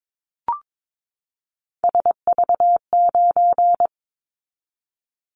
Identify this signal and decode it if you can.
Morse code, heard as a keyed tone, about 22 wpm, 700 Hz. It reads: SV9
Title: Callsign Prefixes for Top 100 DXCC - 22wpm